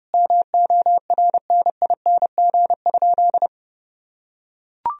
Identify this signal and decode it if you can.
Morse code, heard as a keyed tone, about 30 wpm, 700 Hz. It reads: MORNING?
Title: Sentences from Top 1000 Words - Review Part 2 - 30wpm